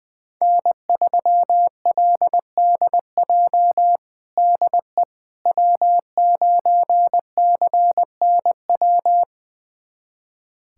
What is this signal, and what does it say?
Morse code, heard as a keyed tone, about 20 wpm, 700 Hz. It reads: N3LDJ DE W9CNW
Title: USA Callsign DE Callsign - 5 Characters - 20wpm